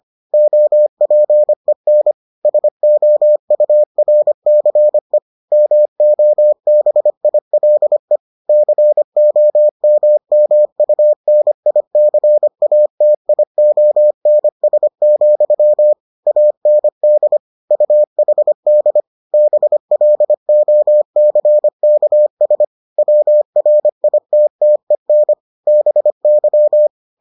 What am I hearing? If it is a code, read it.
Morse code, heard as a keyed tone, about 25 wpm, 600 Hz. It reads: OPEN SOURCE MOBILE COMMUNICATIONS, AND UHD BLOCKS WRITTEN BY